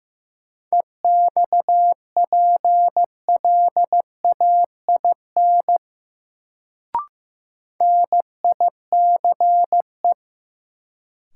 Morse code, heard as a keyed tone, about 15 wpm, 700 Hz. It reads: EXPLAIN NICE
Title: Sentences from Top 1100 Words - New Word Drill Alt - 15wpm